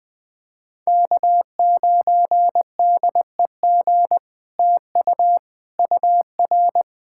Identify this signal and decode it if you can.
Morse code, heard as a keyed tone, about 19 wpm, 700 Hz. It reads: K9DEG TU UR
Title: POTA QSO Practice - 20wpm